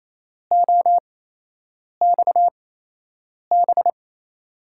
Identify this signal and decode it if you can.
Morse code, heard as a keyed tone, about 27 wpm, spaced as for 8 wpm, 700 Hz. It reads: OXB
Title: USA Call Signs - 5 Characters - 8x Character Spacing - 28wpm